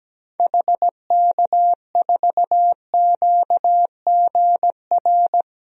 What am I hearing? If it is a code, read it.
Morse code, heard as a keyed tone, about 17 wpm, 700 Hz. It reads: HK4QGR
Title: DXCC-100 Call Signs - 6 Characters - 17wpm